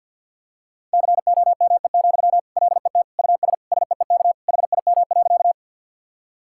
Morse code, heard as a keyed tone, about 50 wpm, 700 Hz. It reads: COME, LET US SEEK HIM.
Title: Sentences from Top 1600 Words 50wpm